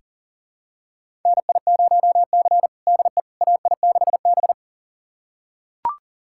Morse code, heard as a keyed tone, about 40 wpm, 700 Hz. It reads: NI0C DE AI6B
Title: USA Callsign DE Callsign - 4 Characters - 40wpm